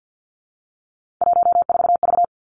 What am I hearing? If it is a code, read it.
Morse code, heard as a keyed tone, about 50 wpm, 700 Hz. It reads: 14V
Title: Three Challenging Letter-Number Combinations 50wpm